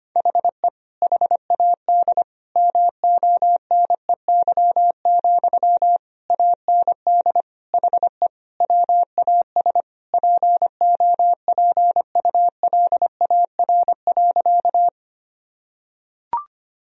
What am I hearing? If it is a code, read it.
Morse code, heard as a keyed tone, about 25 wpm, 700 Hz. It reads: HE HAD MONEY, AND HE WAS POPULAR.
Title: Sentences from Top 1000 Words - Review Part 3 - 25wpm